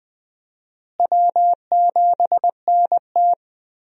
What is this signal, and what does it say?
Morse code, heard as a keyed tone, about 20 wpm, 700 Hz. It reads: W7NT
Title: USA Call Signs - 4 Characters - Repeated 3x - 20wpm